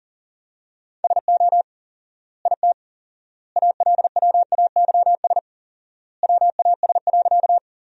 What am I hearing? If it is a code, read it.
Morse code, heard as a keyed tone, about 40 wpm, 700 Hz. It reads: SO IT ALWAYS WAS.
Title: 4x Word-Spacing - Sentences from Top 300 Words - 40wpm